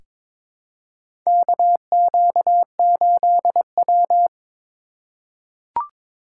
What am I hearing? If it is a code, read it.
Morse code, heard as a keyed tone, about 22 wpm, 700 Hz. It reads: KQ8W
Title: USA Call Signs - 4 Characters - 2x Character Spacing - 22wpm